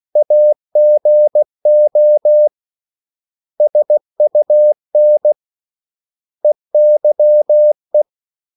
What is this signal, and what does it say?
Morse code, heard as a keyed tone, about 16 wpm, 600 Hz. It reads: AGO SUN EYE
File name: CW / Practice / 500MostCommonEnglishWords16